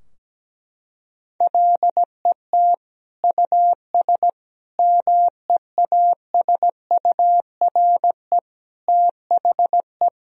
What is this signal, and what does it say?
Morse code, heard as a keyed tone, about 17 wpm, 700 Hz. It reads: LET US MEASURE THE
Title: Sentences from Top 1000 Words - Review Part 2 - 17wpm